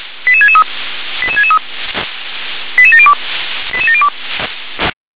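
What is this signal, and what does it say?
Five-tone selective call (ZVEI), heard as quick runs of five tones, tones of about 80 ms. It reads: 88663 88663 88702 88702
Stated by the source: Five-tone ZVEI selective call received on an SDR.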